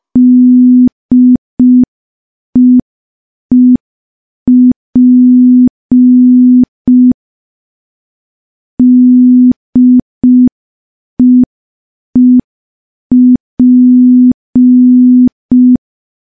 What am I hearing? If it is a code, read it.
Morse code, heard as a keyed tone, about 5 wpm, 250 Hz. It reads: DEEP DEEP
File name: morse / morse-training-05WPM